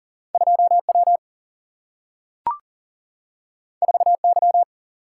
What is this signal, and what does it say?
Morse code, heard as a keyed tone, about 40 wpm, 700 Hz. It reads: 2 W 4 Y
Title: Single-Letter Single-Number 40wpm